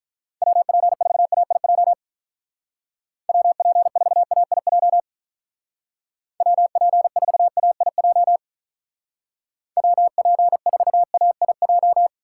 Morse code, heard as a keyed tone, about 43 wpm, 700 Hz. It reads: WP4AIJ WP4AIJ WP4AIJ WP4AIJ
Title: Speed Racing - USA 2x3 Call Signs 35wpm